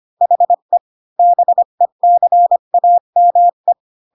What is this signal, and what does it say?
Morse code, heard as a keyed tone, about 25 wpm, 700 Hz. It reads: HE BECAME